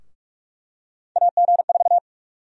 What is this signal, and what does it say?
Morse code, heard as a keyed tone, about 46 wpm, 700 Hz. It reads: AG4
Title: USA Call Sign Prefixes - Repeated 2x - 45wpm